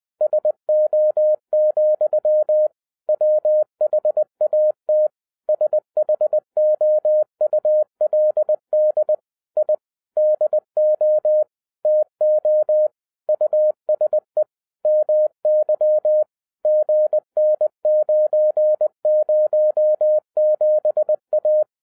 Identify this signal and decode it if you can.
Morse code, heard as a keyed tone, about 20 wpm, 610 Hz. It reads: SO, WHAT SHOULD I DO TO USE MY GN907A